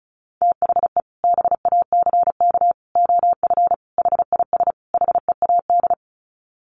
Morse code, heard as a keyed tone, about 34 wpm, 700 Hz. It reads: THE BACK OF HIS HEAD
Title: Sets of 5 Words Encore 35wpm